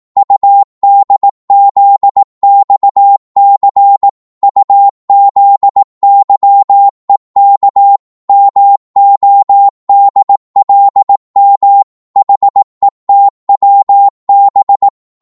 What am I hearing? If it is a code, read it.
Morse code, heard as a keyed tone, about 18 wpm, 800 Hz. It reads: UDZXC UZYEK MODLM HETWB
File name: Morse / ltr-18wpm